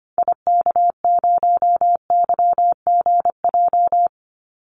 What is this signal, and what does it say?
Morse code, heard as a keyed tone, about 25 wpm, 700 Hz. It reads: IK0YGJ
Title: Call Signs Repeated 2x - 25wpm